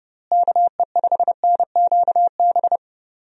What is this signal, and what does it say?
Morse code, heard as a keyed tone, about 31 wpm, 700 Hz. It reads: KE5NQB
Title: USA Call Signs - 6 Characters - Repeated 2x - 30wpm